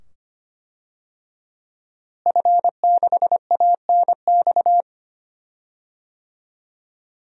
Morse code, heard as a keyed tone, about 25 wpm, 700 Hz. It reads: F6ANX
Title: DXCC-100 Call Signs - 5 Characters - Repeated 2x - 25wpm